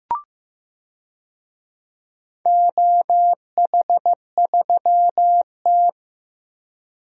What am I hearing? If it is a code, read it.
Morse code, heard as a keyed tone, about 15 wpm, 700 Hz. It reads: OH3T
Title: DXCC-100 Call Signs - 4 Characters - 15wpm